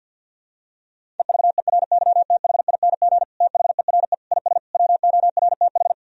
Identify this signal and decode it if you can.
Morse code, heard as a keyed tone, about 52 wpm, 700 Hz. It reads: EVERYTHING THERE IS WORTH
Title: Sentences from Top 1000 Words - Review Part 1 - 50wpm